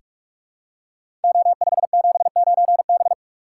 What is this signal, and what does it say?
Morse code, heard as a keyed tone, about 44 wpm, 700 Hz. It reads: O579B
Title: Code Groups - 5 Letter:Numbers - 45wpm